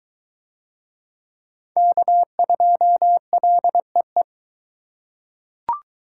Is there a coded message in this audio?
Yes: Morse code from 1.8 to 4.2 s, heard as a keyed tone, about 21 wpm, 700 Hz. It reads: K2LEE